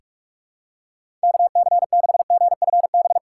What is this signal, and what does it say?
Morse code, heard as a keyed tone, about 46 wpm, 700 Hz. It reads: KC6GFB
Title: USA Call Signs - 6 Characters - 6x Character Spacing - 45wpm